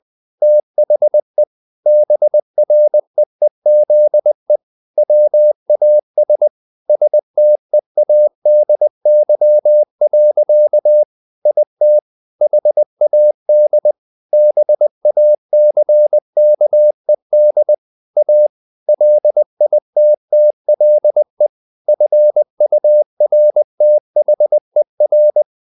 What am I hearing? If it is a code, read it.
Morse code, heard as a keyed tone, about 20 wpm, 600 Hz. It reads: THE BREEZE WAS STEADY. IT HAD BACKED A LITTLE FURTHER
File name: the old man and the sea part 2 20wpm